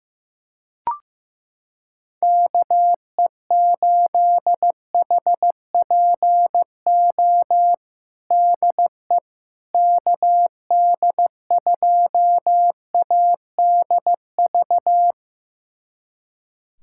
Morse code, heard as a keyed tone, about 16 wpm, 700 Hz. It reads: KE8HPO DE KD2ADV
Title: USA Callsign DE Callsign - 6 Characters - 15wpm